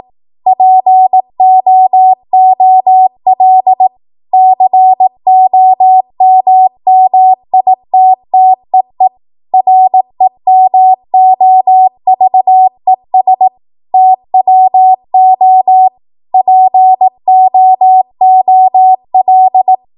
Morse code, heard as a keyed tone, about 18 wpm, 750 Hz. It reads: POOL COMMITTEE REMOVES TWO POOL